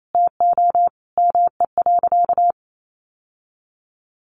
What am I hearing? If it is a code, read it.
Morse code, heard as a keyed tone, about 28 wpm, 700 Hz. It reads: TO ME.
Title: Sentences from Top 1100 Words 28wpm